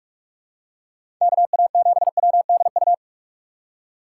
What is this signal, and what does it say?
Morse code, heard as a keyed tone, about 44 wpm, 700 Hz. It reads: KA7WDU
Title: USA Call Signs - 6x Character Spacing - 45wpm